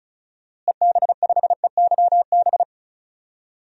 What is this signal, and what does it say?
Morse code, heard as a keyed tone, about 34 wpm, 700 Hz. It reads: EB5EYB
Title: DXCC-100 Call Signs - 6 Characters - Repeated 2x - 35wpm